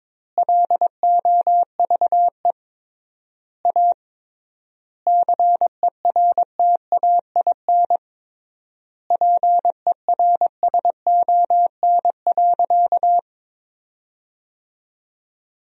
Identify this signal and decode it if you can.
Morse code, heard as a keyed tone, about 22 wpm, 700 Hz. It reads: LOVE A CERTAIN PERSON.
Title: 3x Word-Spacing - Sentences from Top 500 Words - 22wpm